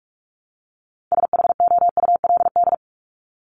Morse code, heard as a keyed tone, about 44 wpm, 700 Hz. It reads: SHOULD
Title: Top 500 Words 45wpm